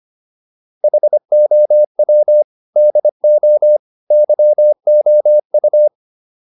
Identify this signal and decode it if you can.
Morse code, heard as a keyed tone, about 25 wpm, 600 Hz. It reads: HOW DO YOU